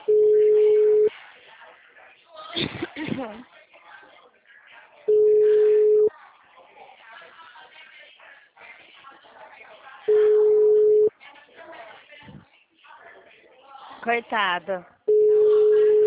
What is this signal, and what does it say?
Ringback tone (Europe, 425 Hz), with speech.